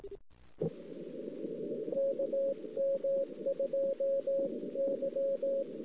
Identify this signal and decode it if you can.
Morse code, heard as a keyed tone, about 18 wpm, 550 Hz. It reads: KM2Y
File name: CW / 2008-ss-cw-k5tr-op-n5rz -01:00